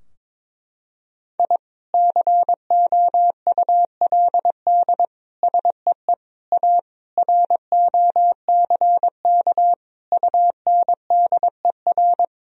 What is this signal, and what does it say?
Morse code, heard as a keyed tone, about 22 wpm, 700 Hz. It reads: I COULD SEE A ROCK UNDER
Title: Sentences from Top 1000 Words - Review Part 2 - 22wpm